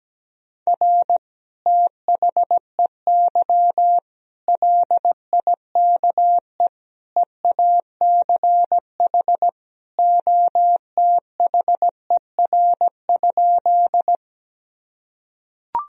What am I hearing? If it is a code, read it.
Morse code, heard as a keyed tone, about 17 wpm, 700 Hz. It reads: R THEY LIKE EACH OTHER?
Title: Sentences from Top 200 Words and Abbreviations 17wpm